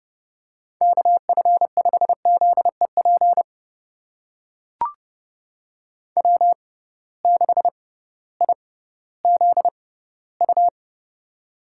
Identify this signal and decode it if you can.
Morse code, heard as a keyed tone, about 30 wpm, spaced as for 16 wpm, 700 Hz. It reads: KF5ZEP W6IZU
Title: USA Call Signs - 6x Character Spacing - 30wpm